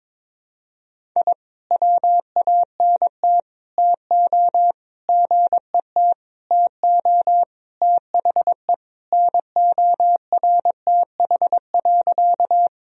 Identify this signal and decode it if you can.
Morse code, heard as a keyed tone, about 22 wpm, 700 Hz. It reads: I WANT TO GET TO THE NORTH.
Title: Sentences from Top 1000 Words - Review Part 1 - 22wpm